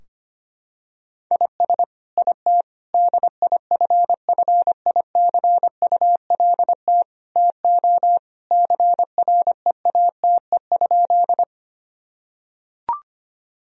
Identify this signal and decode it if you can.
Morse code, heard as a keyed tone, about 25 wpm, 700 Hz. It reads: IS IT DIFFICULT TO CREATE?
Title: Sentences from Top 1000 Words - Review Part 3 - 25wpm